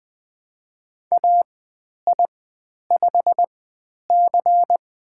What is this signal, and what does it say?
Morse code, heard as a keyed tone, about 20 wpm, spaced as for 10 wpm, 700 Hz. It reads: AI5C
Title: 095 Lesson 16g - Letter c - Call Signs 20wpm